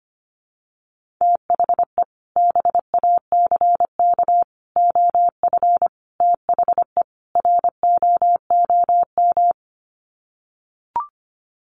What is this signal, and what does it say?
Morse code, heard as a keyed tone, about 25 wpm, 700 Hz. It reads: THE BACK OF THE ROOM
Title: Sets of 5 Words Encore 25wpm